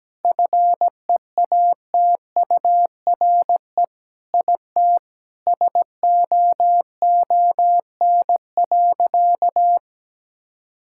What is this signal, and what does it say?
Morse code, heard as a keyed tone, about 17 wpm, 700 Hz. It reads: FEATURE IT SOON.
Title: Sentences from Top 1000 Words - Review Part 5 - 17wpm